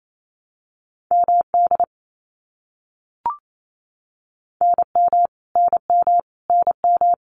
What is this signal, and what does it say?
Morse code, heard as a keyed tone, about 28 wpm, 700 Hz. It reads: MD NM NM NM
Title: US States Repeated 3x 28wpm